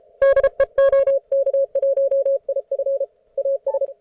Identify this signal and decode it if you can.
Morse code, heard as a keyed tone, about 35 wpm, 550 Hz. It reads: DE OK1IF AS